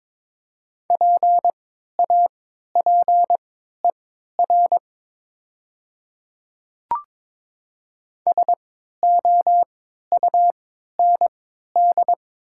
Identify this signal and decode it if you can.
Morse code, heard as a keyed tone, about 22 wpm, spaced as for 12 wpm, 700 Hz. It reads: PAPER SOUND
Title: 3x Character Spacing - Top 500 Words - 22wpm